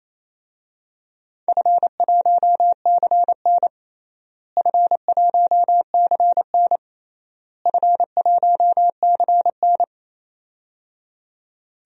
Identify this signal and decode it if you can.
Morse code, heard as a keyed tone, about 28 wpm, 700 Hz. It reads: F1CN F1CN F1CN
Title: DXCC-100 Call Signs - 4 Characters - Repeated 3x - 28wpm